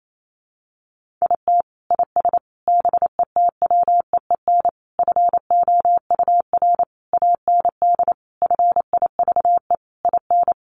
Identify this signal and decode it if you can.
Morse code, heard as a keyed tone, about 28 wpm, 700 Hz. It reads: IT IS BETWEEN FOUR AND FIVE IN
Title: Sentences from Top 1500 Words 28wpm